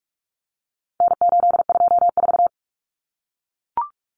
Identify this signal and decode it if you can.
Morse code, heard as a keyed tone, about 44 wpm, 700 Hz. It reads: N824
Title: Common Cut Numbers - 4-Digit 45wpm